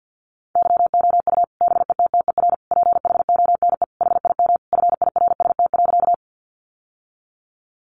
Morse code, heard as a keyed tone, about 50 wpm, 700 Hz. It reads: YOU BETTER PHONE HIM FIRST.
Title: Sentences from Top 1100 Words 50wpm